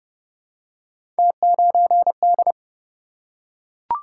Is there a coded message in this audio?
Yes: Morse code from 1.2 to 2.5 s, heard as a keyed tone, about 28 wpm, 700 Hz. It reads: T9D